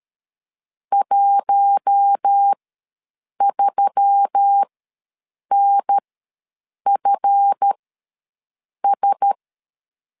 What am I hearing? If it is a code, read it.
Morse code, heard as a keyed tone, about 13 wpm, spaced as for 7 wpm, 800 Hz. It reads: 13NFS
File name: LEZIONI / 7 GF9Z / 7 GF ws 12wpm 3140